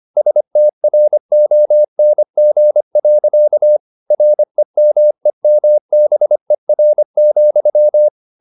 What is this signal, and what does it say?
Morse code, heard as a keyed tone, about 25 wpm, 600 Hz. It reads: STRONG. REMEMBER,
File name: the old man and the sea part 1 25wpm